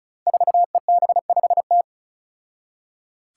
Morse code, heard as a keyed tone, about 36 wpm, 700 Hz. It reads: 4EB5T
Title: Five Challenging Letter-Number Combinations 35wpm